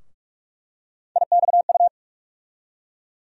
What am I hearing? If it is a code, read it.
Morse code, heard as a keyed tone, about 42 wpm, 700 Hz. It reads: IXU
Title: USA Call Sign Postfixes - 3 Letters 45wpm